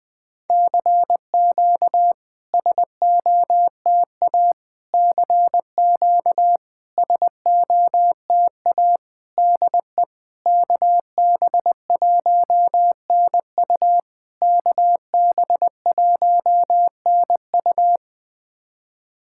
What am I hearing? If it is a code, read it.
Morse code, heard as a keyed tone, about 20 wpm, 690 Hz. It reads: CQ SOTA CQ SOTA DE KB1NU KB1NU
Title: SOTA QSO Practice 20wpm